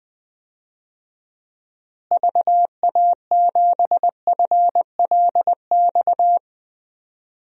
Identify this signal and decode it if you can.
Morse code, heard as a keyed tone, about 21 wpm, 700 Hz. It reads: VA7FLX
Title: DXCC-100 Call Signs - 6 Characters - Repeated 2x - 20wpm